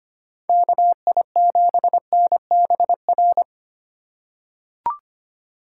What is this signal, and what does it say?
Morse code, heard as a keyed tone, about 25 wpm, 700 Hz. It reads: KI7NBR
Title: USA Call Signs - 6 Characters - 3x Character Spacing - 25wpm